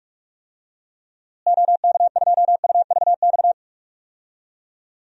Morse code, heard as a keyed tone, about 42 wpm, 700 Hz. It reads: OK2UVX
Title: DXCC-100 Call Signs - 6 Characters - Repeated 3x - 45wpm